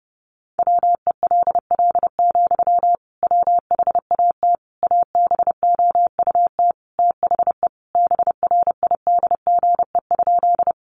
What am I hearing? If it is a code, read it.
Morse code, heard as a keyed tone, about 30 wpm, 700 Hz. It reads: WELL, WHAT ABOUT THE BRIDGE?
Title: Sentences from Top 1200 Words 30wpm